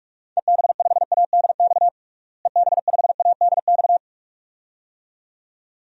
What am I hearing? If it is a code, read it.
Morse code, heard as a keyed tone, about 42 wpm, 690 Hz. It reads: EB5ADX EB5ADX
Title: DXCC-100 Call Signs - 6 Characters - Repeated 2x - 45wpm